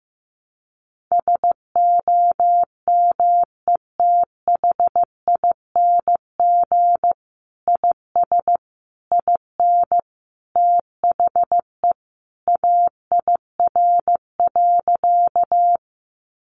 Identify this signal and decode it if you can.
Morse code, heard as a keyed tone, about 15 wpm, 700 Hz. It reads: SOMETHING IS IN THE AIR.
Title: Sentences from Top 600 Words 15wpm